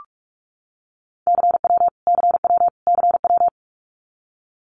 Morse code, heard as a keyed tone, about 44 wpm, 700 Hz. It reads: CW CW CW
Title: Abbreviations Repeated 3x - 45wpm